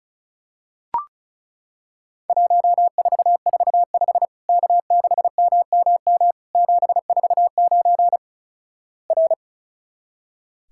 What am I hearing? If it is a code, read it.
Morse code, heard as a keyed tone, about 36 wpm, 700 Hz. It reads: 1445 K6MMM 749 R
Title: Worked All Europe DX Contest - QTC Exchange - 35wpm